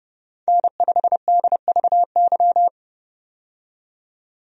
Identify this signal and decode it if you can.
Morse code, heard as a keyed tone, about 30 wpm, 700 Hz. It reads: N5DVY
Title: USA Call Signs - 5 Characters - 30wpm